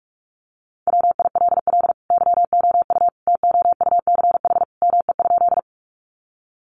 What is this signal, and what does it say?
Morse code, heard as a keyed tone, about 44 wpm, 700 Hz. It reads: WILL YOU TOUCH ME?
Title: Sentences from Top 1200 Words 45wpm